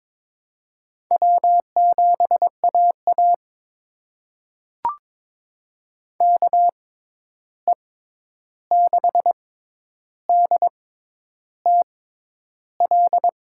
Morse code, heard as a keyed tone, about 22 wpm, spaced as for 10 wpm, 700 Hz. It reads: W7AA KE6DTL
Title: USA Call Signs - 6x Character Spacing - 22wpm